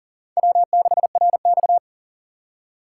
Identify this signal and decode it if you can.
Morse code, heard as a keyed tone, about 40 wpm, 700 Hz. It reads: W6RX
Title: USA Call Signs - 4 Characters - 2x Character Spacing - 40wpm